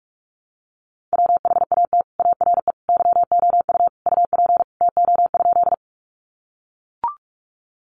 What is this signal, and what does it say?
Morse code, heard as a keyed tone, about 44 wpm, 700 Hz. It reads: WHAT ARE YOU UP TO?
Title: Sentences from Top 100 Words 45wpm